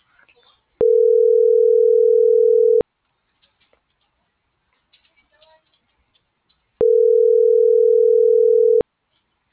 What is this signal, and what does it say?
Ringback tone (North America), 440 + 480 Hz.